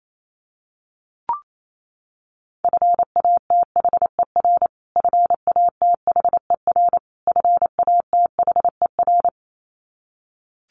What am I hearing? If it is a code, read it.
Morse code, heard as a keyed tone, about 28 wpm, 700 Hz. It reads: FATHER FATHER FATHER
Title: Top 500 Words Repeated 28wpm